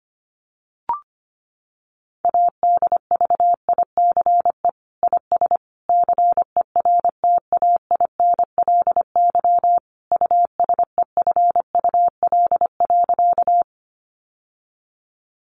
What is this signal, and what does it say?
Morse code, heard as a keyed tone, about 25 wpm, 700 Hz. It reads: ADVICE IS CERTAINLY USEFUL.
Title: Sentences from Top 1100 Words 25wpm